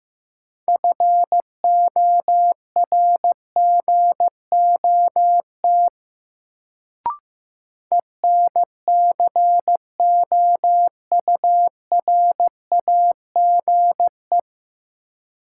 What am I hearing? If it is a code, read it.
Morse code, heard as a keyed tone, about 15 wpm, 700 Hz. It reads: FORGOT ENCOURAGE
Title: Sentences from Top 1400 Words - New Word Drill - 15wpm